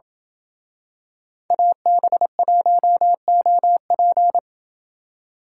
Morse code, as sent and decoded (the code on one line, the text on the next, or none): .- -... .---- --- .--.
AB1OP